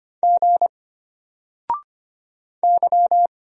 Morse code, heard as a keyed tone, about 25 wpm, 700 Hz. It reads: G Y
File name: Single-Letter-Number-25wpm